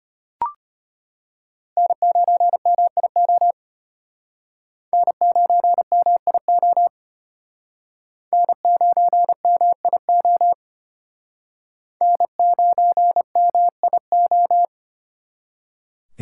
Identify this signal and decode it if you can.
Morse code, heard as a keyed tone, about 32 wpm, 700 Hz. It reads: N9MIO N9MIO N9MIO N9MIO